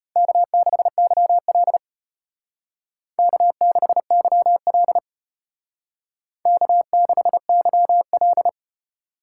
Morse code, heard as a keyed tone, about 32 wpm, 700 Hz. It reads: K6YL K6YL K6YL